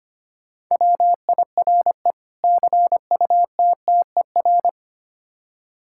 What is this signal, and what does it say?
Morse code, heard as a keyed tone, about 25 wpm, 700 Hz. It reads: WIRE CUTTER
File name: practice / 25wpm / Difficult Words with Hints - 25wpm